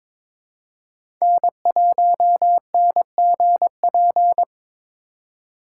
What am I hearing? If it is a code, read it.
Morse code, heard as a keyed tone, about 22 wpm, 700 Hz. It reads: N1NGP
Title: USA Call Signs - 5 Characters - 4x Character Spacing - 22wpm